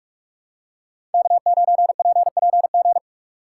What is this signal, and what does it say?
Morse code, heard as a keyed tone, about 46 wpm, 700 Hz. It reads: K9PPG
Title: USA Call Signs - 5 Characters - 4x Character Spacing - 45wpm